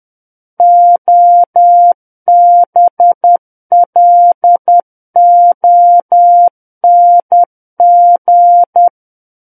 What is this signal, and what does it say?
Morse code, heard as a keyed tone, about 10 wpm, 700 Hz. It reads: OBLONG
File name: Book One / 01-Chapter 1 - 10 WPM